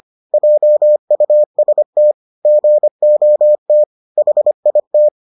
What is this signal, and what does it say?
Morse code, heard as a keyed tone, about 25 wpm, 600 Hz. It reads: JUST GOT HIT